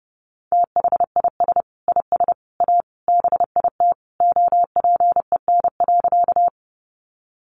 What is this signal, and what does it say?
Morse code, heard as a keyed tone, about 30 wpm, 700 Hz. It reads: THIS IS A BIT OPEN.
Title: Sentences from Top 900 Words 30wpm